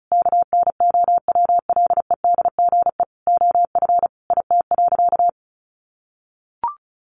Morse code, heard as a keyed tone, about 34 wpm, 700 Hz. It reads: KNOWLEDGE OF IT.
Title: Sentences from Top 1400 Words 35wpm